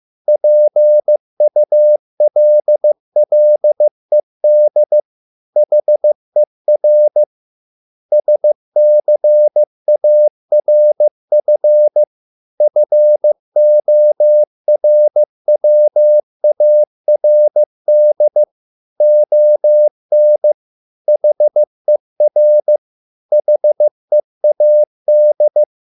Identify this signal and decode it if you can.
Morse code, heard as a keyed tone, about 15 wpm, 600 Hz. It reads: PULLED HER SCARF FORWARD ON HER HEAD